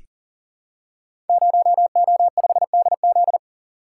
Morse code, heard as a keyed tone, about 40 wpm, 700 Hz. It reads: 0O5DZ
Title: Code Groups - 5 Letter:Numbers - 40wpm